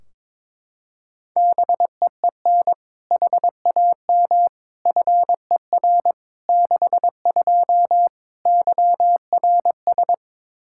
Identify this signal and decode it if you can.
Morse code, heard as a keyed tone, about 22 wpm, 700 Hz. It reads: BEEN HAM FER 62 YRS
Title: K1USN SST QSO Practice 22wpm